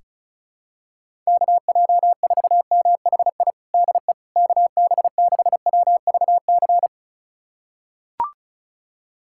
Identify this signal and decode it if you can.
Morse code, heard as a keyed tone, about 34 wpm, 700 Hz. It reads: KJ4MHI DE KB6WVC
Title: USA Callsign DE Callsign - 6 Characters - 35wpm